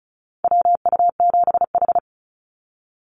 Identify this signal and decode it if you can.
Morse code, heard as a keyed tone, about 34 wpm, 700 Hz. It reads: WU7H
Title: NA SOTA Activator Callsigns 35wpm